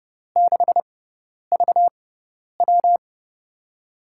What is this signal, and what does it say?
Morse code, heard as a keyed tone, about 30 wpm, spaced as for 10 wpm, 700 Hz. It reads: 6VW
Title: USA Call Signs - 4 Characters - 6x Character Spacing - 30wpm